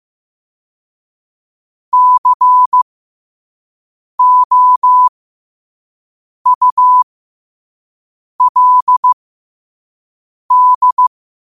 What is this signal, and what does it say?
Morse code, heard as a keyed tone, about 15 wpm, spaced as for 5 wpm, 1000 Hz. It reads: COULD
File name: qotdfiles / 5wpm